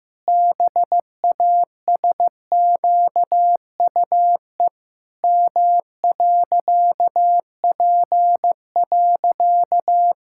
Morse code, heard as a keyed tone, about 15 wpm, 700 Hz. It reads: BASQUE M.P.
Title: Pangram Challenge v2 - 15wpm